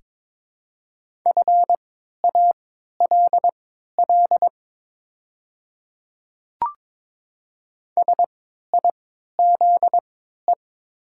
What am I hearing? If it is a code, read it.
Morse code, heard as a keyed tone, about 22 wpm, spaced as for 13 wpm, 700 Hz. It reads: FALL SIZE
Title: 3x Character Spacing - Top 500 Words - 22wpm